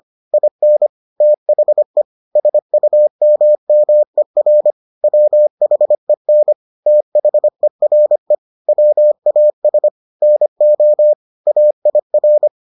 Morse code, heard as a keyed tone, about 25 wpm, 600 Hz. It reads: IN THE SUMMER WHEN THERE WAS NO AIR